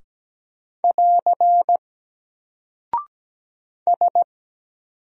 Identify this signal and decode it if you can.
Morse code, heard as a keyed tone, about 17 wpm, 700 Hz. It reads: <AR> S
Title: Single Character (Letter:Number:Punctuation:Prosign) 17wpm